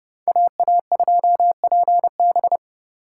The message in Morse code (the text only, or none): AA2PB